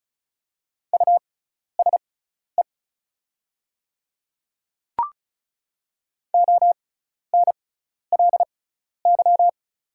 Morse code, heard as a keyed tone, about 36 wpm, spaced as for 12 wpm, 700 Hz. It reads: USE ONLY
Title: 6x Character Spacing - Top 200 Words - 35wpm